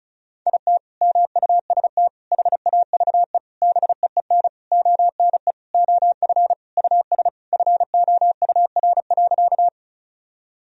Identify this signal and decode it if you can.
Morse code, heard as a keyed tone, about 36 wpm, 700 Hz. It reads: IT MUST HAVE BEEN ONE OF US FOUR.
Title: Sentences from Top 1000 Words - Review Part 1 - 35wpm